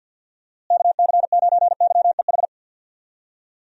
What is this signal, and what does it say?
Morse code, heard as a keyed tone, about 51 wpm, 700 Hz. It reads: KC9YEH
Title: USA Call Signs - 6 Characters - Repeated 2x - 50wpm